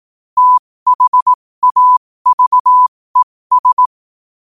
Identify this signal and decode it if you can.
Morse code, heard as a keyed tone, about 18 wpm, spaced as for 15 wpm, 1000 Hz. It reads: THAVES
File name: qotdfiles / 15wpm